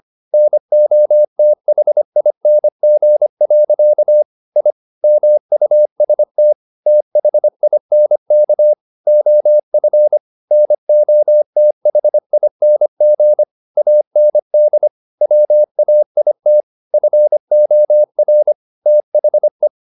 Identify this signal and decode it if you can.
Morse code, heard as a keyed tone, about 25 wpm, 600 Hz. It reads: NOTHING. I MUST THINK OF NOTHING AND WAIT FOR THE